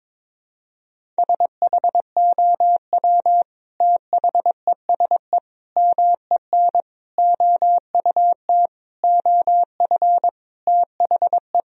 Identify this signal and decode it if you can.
Morse code, heard as a keyed tone, about 22 wpm, 700 Hz. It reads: SHOW THESE MEN OUT OF THE
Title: Sentences from Top 1000 Words - Review Part 1 - 22wpm